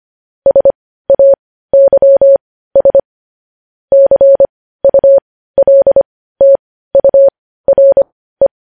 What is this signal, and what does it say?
Morse code, heard as a keyed tone, about 25 wpm, spaced as for 15 wpm, 550 Hz. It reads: SAYS CULTURE
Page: news-headlines-in-morse-code-at-15-wpm